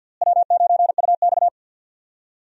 Morse code, heard as a keyed tone, about 46 wpm, 700 Hz. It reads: W9UX
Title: USA Call Signs - 4 Characters - Repeated 3x - 50wpm